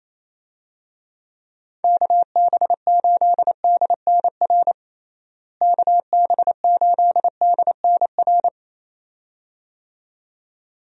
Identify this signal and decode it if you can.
Morse code, heard as a keyed tone, about 28 wpm, 700 Hz. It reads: KB8DNR KB8DNR
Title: USA Call Signs - 6 Characters - Repeated 2x - 28wpm